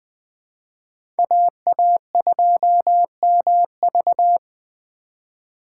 Morse code, heard as a keyed tone, about 21 wpm, 700 Hz. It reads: AA2MV